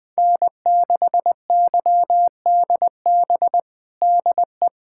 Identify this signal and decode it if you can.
Morse code, heard as a keyed tone, about 20 wpm, 700 Hz. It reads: N6YDB DE
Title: USA Callsign DE Callsign - 5 Characters - 20wpm